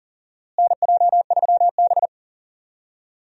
Morse code, heard as a keyed tone, about 40 wpm, 700 Hz. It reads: NJ3B
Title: USA Call Signs - 4 Characters - Repeated 2x - 40wpm